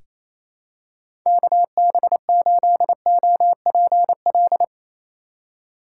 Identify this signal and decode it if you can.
Morse code, heard as a keyed tone, about 27 wpm, 700 Hz. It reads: KB8OPL
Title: USA Call Signs - 6 Characters - 3x Character Spacing - 28wpm